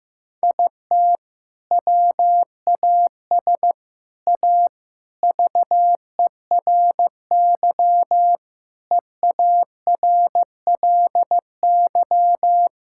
Morse code, heard as a keyed tone, about 15 wpm, 700 Hz. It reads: IT WAS A VERY EARLY
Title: Sentences from Top 1000 Words - Review Part 6 - 15wpm